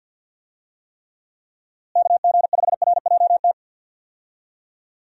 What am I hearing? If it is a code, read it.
Morse code, heard as a keyed tone, about 51 wpm, 700 Hz. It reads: KG5RJT
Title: USA Call Signs - 6 Characters - Repeated 2x - 50wpm